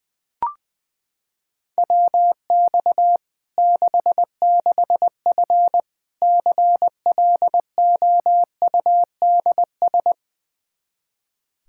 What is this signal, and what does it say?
Morse code, heard as a keyed tone, about 21 wpm, 700 Hz. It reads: WX 66F CLOUDS
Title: Practice QSO - Exchange 2 - Info - 20wpm